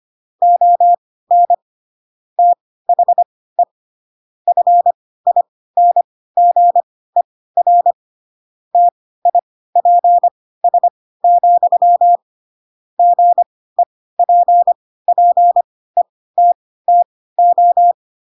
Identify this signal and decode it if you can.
Morse code, heard as a keyed tone, about 25 wpm, spaced as for 16 wpm, 700 Hz. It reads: ON THE FINGER TIPS, GEPPETTO
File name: Pinocchio / Pinocchio ch3 10 25 ik1hge